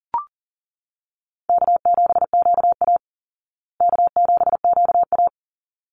Morse code, heard as a keyed tone, about 40 wpm, 700 Hz. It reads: K7QA K7QA
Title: Easy Call Signs Repeated 2x - 40wpm